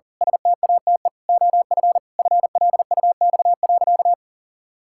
Morse code, heard as a keyed tone, about 40 wpm, 700 Hz. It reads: STATE OF FLUX.
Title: Pangram Challenge v2 - 40wpm